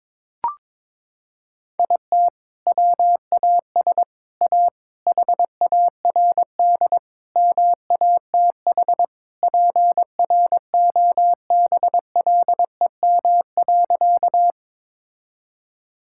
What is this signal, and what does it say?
Morse code, heard as a keyed tone, about 22 wpm, 700 Hz. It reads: IT WAS A HARD MATH PROBLEM.
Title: Sentences from Top 1000 Words - Review Part 6 - 22wpm